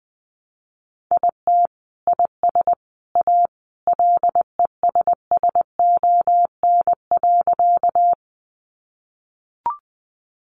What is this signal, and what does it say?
Morse code, heard as a keyed tone, about 20 wpm, 700 Hz. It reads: IT IS A LESSON.
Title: Sentences from Top 1200 Words 20wpm